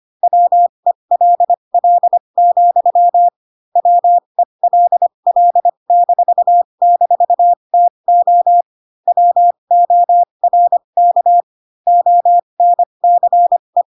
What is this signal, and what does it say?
Morse code, heard as a keyed tone, about 25 wpm, spaced as for 22 wpm, 700 Hz. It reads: WELL, WELL--TO WORK ONCE